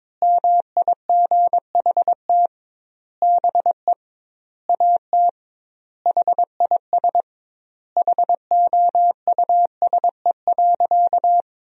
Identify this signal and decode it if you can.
Morse code, heard as a keyed tone, about 22 wpm, 700 Hz. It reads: MIGHT BE AT HIS HOUSE.
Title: 2x Word-Spacing - Sentences from Top 300 Words - 22wpm